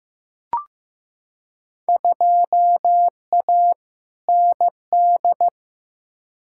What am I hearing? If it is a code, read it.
Morse code, heard as a keyed tone, about 15 wpm, 700 Hz. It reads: 2A ND
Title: Field Day Exchanges - 15wpm